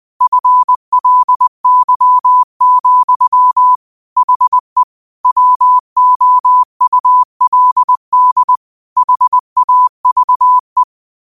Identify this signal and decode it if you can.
Morse code, heard as a keyed tone, about 20 wpm, 1000 Hz. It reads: FLY, HE WOULD HAVE
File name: qotdfiles / 20wpm